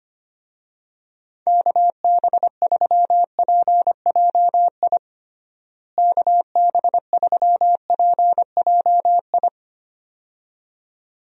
Morse code, heard as a keyed tone, about 25 wpm, 700 Hz. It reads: KB3PJI KB3PJI
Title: USA Call Signs - 6 Characters - Repeated 2x - 25wpm